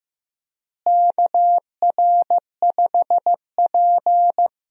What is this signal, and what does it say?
Morse code, heard as a keyed tone, about 15 wpm, 700 Hz. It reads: KR5P
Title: USA Call Signs - 4 Characters - Repeated 3x - 15wpm